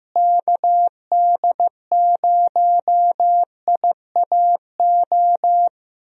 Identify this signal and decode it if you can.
Morse code, heard as a keyed tone, about 15 wpm, 700 Hz. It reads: KD0IAO
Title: USA Call Signs - 6 Characters - Repeated 3x - 15wpm